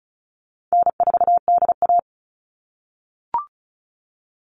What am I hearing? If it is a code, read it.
Morse code, heard as a keyed tone, about 34 wpm, 700 Hz. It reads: N4DA
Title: NA SOTA Chaser Callsigns 35wpm